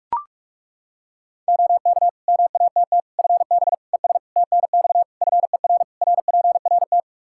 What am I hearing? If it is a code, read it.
Morse code, heard as a keyed tone, about 41 wpm, 700 Hz. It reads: OK MATT FB ES TNX FER RPRT
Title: Practice QSO - Exchange 2 - Info - 45wpm